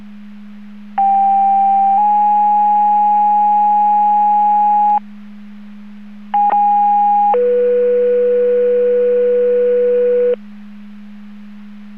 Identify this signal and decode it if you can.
Two-tone page, heard as a one-second tone, then a three-second tone, about 790 then 830 Hz, about 820 then 490 Hz. It is heard from 1.0 to 10.3 s.